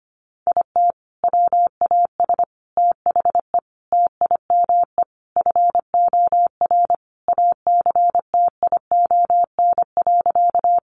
Morse code, heard as a keyed tone, about 25 wpm, 700 Hz. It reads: IT WAS THE TIME FOR ACTION.
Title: Sentences from Top 700 Words 25wpm